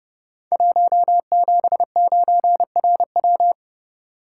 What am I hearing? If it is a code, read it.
Morse code, heard as a keyed tone, about 30 wpm, 700 Hz. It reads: 179RW
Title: Code Groups - 5 Letter:Numbers - 30wpm